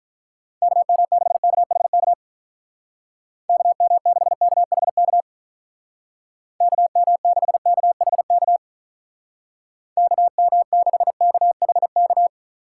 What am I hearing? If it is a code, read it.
Morse code, heard as a keyed tone, about 43 wpm, 700 Hz. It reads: KM6KHK KM6KHK KM6KHK KM6KHK
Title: Speed Racing - USA 2x3 Call Signs 35wpm